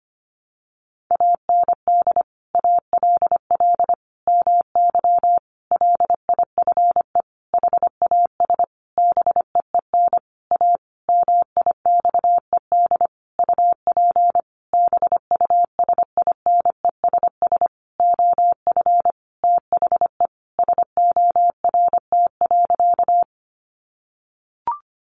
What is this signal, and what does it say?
Morse code, heard as a keyed tone, about 25 wpm, 700 Hz. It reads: AND ALL MY LIFE HAS BEEN A MIXED UP BUSINESS OF THE SORT.
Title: Sentences from Top 1500 Words 25wpm